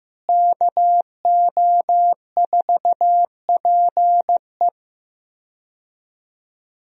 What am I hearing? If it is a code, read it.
Morse code, heard as a keyed tone, about 15 wpm, 700 Hz. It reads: KO4PE
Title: USA Call Signs - 5 Characters - Repeated 2x - 15wpm